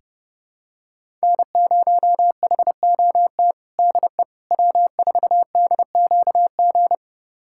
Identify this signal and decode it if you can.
Morse code, heard as a keyed tone, about 30 wpm, 700 Hz. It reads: N0HOT DE W4DQG
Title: USA Callsign DE Callsign - 5 Characters - 30wpm